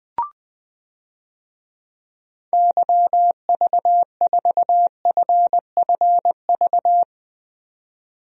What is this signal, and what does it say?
Morse code, heard as a keyed tone, about 21 wpm, 700 Hz. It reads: YV4FFV
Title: DXCC-100 Call Signs - 6 Characters - 20wpm